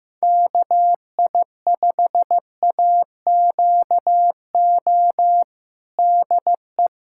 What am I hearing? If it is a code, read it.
Morse code, heard as a keyed tone, about 15 wpm, 700 Hz. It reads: KI5AQO DE
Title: USA Callsign DE Callsign 15wpm